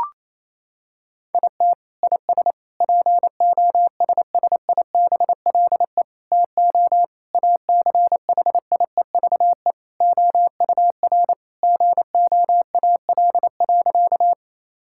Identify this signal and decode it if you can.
Morse code, heard as a keyed tone, about 28 wpm, 700 Hz. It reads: IT IS POSSIBLE TO ACHIEVE OUR GOAL.
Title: Sentences from Top 1000 Words - Review Part 6 - 28wpm